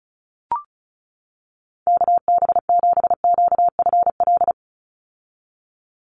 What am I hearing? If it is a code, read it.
Morse code, heard as a keyed tone, about 34 wpm, 700 Hz. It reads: KB7QFL
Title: USA Call Signs 35wpm